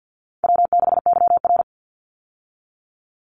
Morse code, heard as a keyed tone, about 49 wpm, 700 Hz. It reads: W6YR
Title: Call Signs Repeated 3x - 50wpm